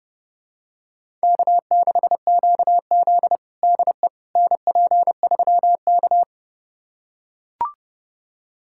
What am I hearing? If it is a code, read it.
Morse code, heard as a keyed tone, about 30 wpm, 700 Hz. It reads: K6QZ DE NP3K
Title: USA Callsign DE Callsign - 4 Characters - 30wpm